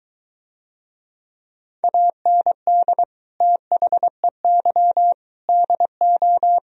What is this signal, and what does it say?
Morse code, heard as a keyed tone, about 23 wpm, 700 Hz. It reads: AND THEY DO